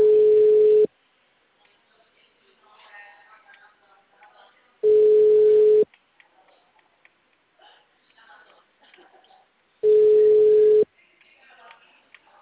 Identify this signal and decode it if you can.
Ringback tone (Europe, 425 Hz), with speech.
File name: out- -2004- - - .57998